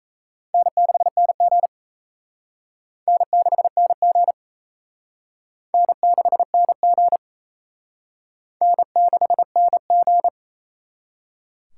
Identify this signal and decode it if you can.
Morse code, heard as a keyed tone, about 34 wpm, 700 Hz. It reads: N6NG N6NG N6NG N6NG
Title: Speed Racing - USA 1x2 Call Signs 28wpm